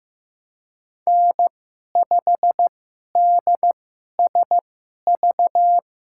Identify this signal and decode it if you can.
Morse code, heard as a keyed tone, about 15 wpm, spaced as for 11 wpm, 700 Hz. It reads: N5DSV
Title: USA Call Signs - 2x Character Spacing - 15wpm